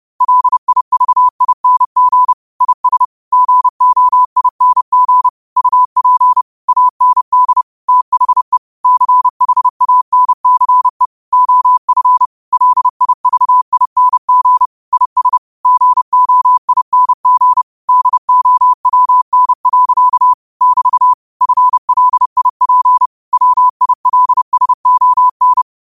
Morse code, heard as a keyed tone, about 30 wpm, 1000 Hz. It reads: LIVING IS GOING UP AND THE CHANCE OF LIVING IS GOING DOWN. <BT> FLIP WILSON
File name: qotdfiles / 30wpm